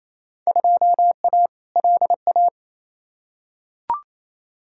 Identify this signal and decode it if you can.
Morse code, heard as a keyed tone, about 28 wpm, 700 Hz. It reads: 2A LA
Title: Field Day Exchanges Encore - 28wpm